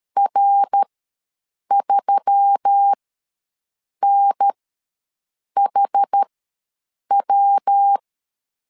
Morse code, heard as a keyed tone, about 13 wpm, spaced as for 7 wpm, 800 Hz. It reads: R3NHW
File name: LEZIONI / 6 PKX0 / 6 PKX ws 12wpm 3106